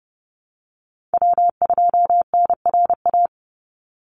Morse code, heard as a keyed tone, about 30 wpm, 700 Hz. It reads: W2NRA
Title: Call Signs - 30wpm